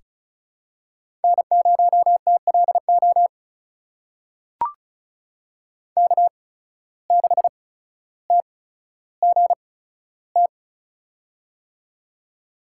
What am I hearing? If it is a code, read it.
Morse code, heard as a keyed tone, about 34 wpm, spaced as for 15 wpm, 700 Hz. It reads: N0TLO K6TGT
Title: USA Call Signs - 5 Characters - 8x Character Spacing - 35wpm